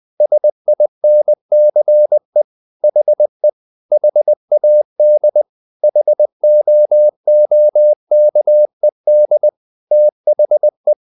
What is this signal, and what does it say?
Morse code, heard as a keyed tone, about 20 wpm, 600 Hz. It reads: SINCE HE HAD HOOKED THE
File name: the old man and the sea / the old man and the sea part 1 20wpm